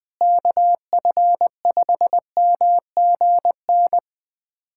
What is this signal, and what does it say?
Morse code, heard as a keyed tone, about 21 wpm, 700 Hz. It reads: KF5MGN
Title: USA Callsign DE Callsign - 6 Characters - 20wpm